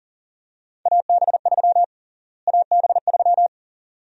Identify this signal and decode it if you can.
Morse code, heard as a keyed tone, about 40 wpm, 700 Hz. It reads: AB3 AB3
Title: USA Call Sign Prefixes - Repeated 2x - 40wpm